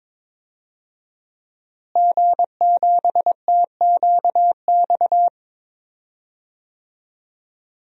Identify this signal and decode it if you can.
Morse code, heard as a keyed tone, about 21 wpm, 700 Hz. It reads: G7TQX
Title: DXCC-100 Call Signs - 5 Characters - Repeated 3x - 22wpm